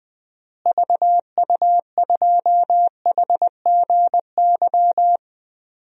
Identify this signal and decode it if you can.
Morse code, heard as a keyed tone, about 21 wpm, 700 Hz. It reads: VU2HGY
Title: DXCC-100 Call Signs - 6 Characters - Repeated 3x - 20wpm